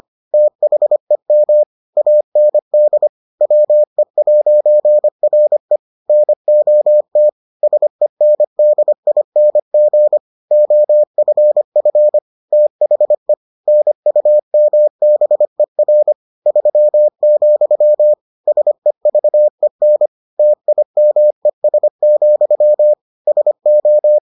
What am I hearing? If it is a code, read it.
Morse code, heard as a keyed tone, about 25 wpm, 600 Hz. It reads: THEM AND WE'RE NOT SENDING OFF THE NUMBER 3, SEVEN TIMES, SO